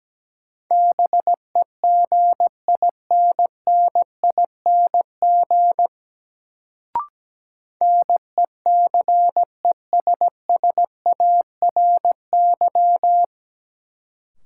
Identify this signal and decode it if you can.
Morse code, heard as a keyed tone, about 17 wpm, 700 Hz. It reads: BEGINNING NECESSARY
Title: Sentences from Top 800 Words - New Word Drill - 17wpm